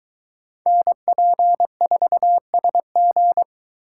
Morse code, heard as a keyed tone, about 22 wpm, 700 Hz. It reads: NP4SG